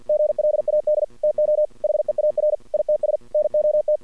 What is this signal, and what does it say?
Morse code, heard as a keyed tone, about 48 wpm, 610 Hz. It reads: GLAD TO HEAR ITS NOT
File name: archived / CRSnet